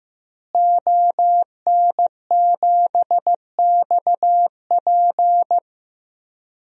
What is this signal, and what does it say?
Morse code, heard as a keyed tone, about 15 wpm, 700 Hz. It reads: ON7XP
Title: DXCC-100 Call Signs - 5 Characters - Repeated 2x - 15wpm